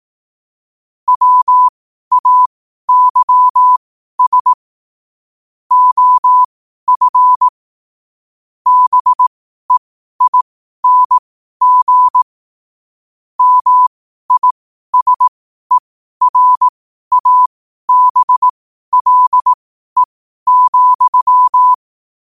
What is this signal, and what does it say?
Morse code, heard as a keyed tone, about 18 wpm, spaced as for 12 wpm, 1000 Hz. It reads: WAYS OF BEING MISERABLE,
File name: qotdfiles / 12wpm